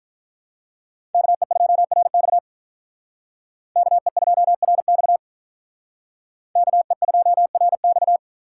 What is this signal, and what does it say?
Morse code, heard as a keyed tone, about 43 wpm, 700 Hz. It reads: KE2RX KE2RX KE2RX
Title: Speed Racing - USA 2x2 Call Signs 35wpm